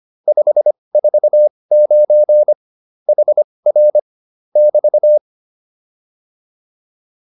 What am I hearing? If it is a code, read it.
Morse code, heard as a keyed tone, about 25 wpm, spaced as for 20 wpm, 600 Hz. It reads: 549 HR <BT>